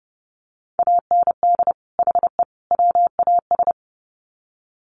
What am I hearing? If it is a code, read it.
Morse code, heard as a keyed tone, about 30 wpm, 700 Hz. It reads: AND HE WAS
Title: Sets of 3 Words 30wpm